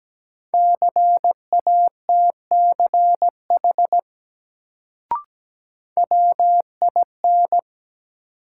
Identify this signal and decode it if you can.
Morse code, heard as a keyed tone, about 17 wpm, 700 Hz. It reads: CATCH WIN
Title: Sentences from Top 1500 Words - New Word Drill Alt - 17wpm